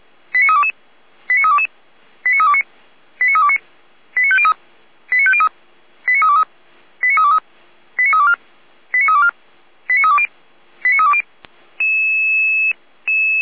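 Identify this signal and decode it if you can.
Five-tone selective call (ZVEI), heard as quick runs of five tones, tones of about 70 ms. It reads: 89322 89322 89329 89329 89503 89503 89323 89323 89324 89324 89320 89320
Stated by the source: ZVEI five-tone selective call received by SDR, from the Feuerwehr Orlamünde (fire brigade) audio server.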